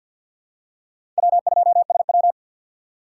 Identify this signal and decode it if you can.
Morse code, heard as a keyed tone, about 47 wpm, 700 Hz. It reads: W2SW
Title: USA Call Signs - 4 Characters - 3x Character Spacing - 50wpm